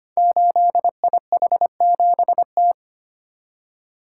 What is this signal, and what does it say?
Morse code, heard as a keyed tone, about 25 wpm, 700 Hz. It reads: 8IH7T
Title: Code Groups - 5 Letter:Numbers - 25wpm